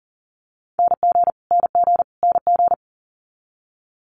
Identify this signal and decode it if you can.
Morse code, heard as a keyed tone, about 39 wpm, 690 Hz. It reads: NG NG NG
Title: Multi-letter Phonemes Repeated 3x 40wpm